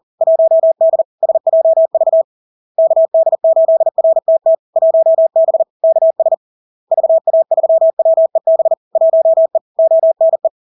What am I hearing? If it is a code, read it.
Morse code, heard as a keyed tone, about 40 wpm, 650 Hz. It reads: 1D SJV KD8RTT 1B KS VA3WEB 1E ONE
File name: LARK-FD-2022-Log-in-CW-40wpm-650-Hz